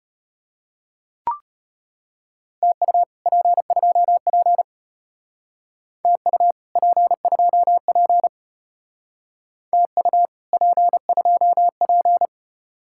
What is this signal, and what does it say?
Morse code, heard as a keyed tone, about 33 wpm, 700 Hz. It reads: TU P2P TU P2P TU P2P